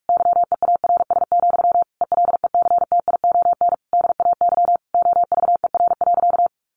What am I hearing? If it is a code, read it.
Morse code, heard as a keyed tone, about 44 wpm, 700 Hz. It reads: YEARS, ELECTION DAY OVER.
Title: A Sound Of Thunder 45wpm